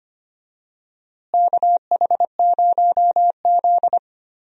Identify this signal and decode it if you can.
Morse code, heard as a keyed tone, about 25 wpm, 700 Hz. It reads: KH0Z
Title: USA Call Signs - 4 Characters - Repeated 3x - 25wpm